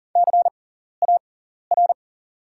Morse code, heard as a keyed tone, about 40 wpm, spaced as for 14 wpm, 700 Hz. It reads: CAR
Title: 6x Character Spacing - Top 500 Words - 40wpm